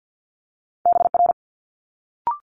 Morse code, heard as a keyed tone, about 50 wpm, 700 Hz. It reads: BR
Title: Multi-letter Phonemes Quick Fire 50wpm